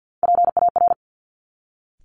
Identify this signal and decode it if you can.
Morse code, heard as a keyed tone, about 50 wpm, 700 Hz. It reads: PAR
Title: Top Three Letter Combinations in Words 50wpm